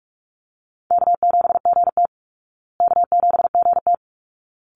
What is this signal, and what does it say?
Morse code, heard as a keyed tone, about 44 wpm, 700 Hz. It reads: K7GT K7GT
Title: Easy Call Signs Repeated 2x - 45wpm